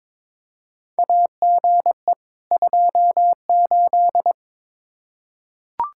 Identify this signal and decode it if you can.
Morse code, heard as a keyed tone, about 22 wpm, 700 Hz. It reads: AGE 28
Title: K1USN SST QSO Practice 22wpm